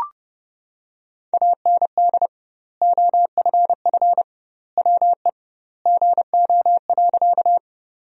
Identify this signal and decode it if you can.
Morse code, heard as a keyed tone, about 30 wpm, 700 Hz. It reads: AND OFF WE GO.
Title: 2x Word-Spacing - Sentences from Top 200 Words - 30wpm